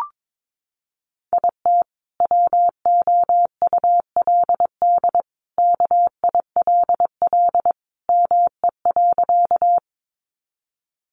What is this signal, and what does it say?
Morse code, heard as a keyed tone, about 22 wpm, 700 Hz. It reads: IT WOULD KILL ME.
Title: Sentences from Top 1400 Words 22wpm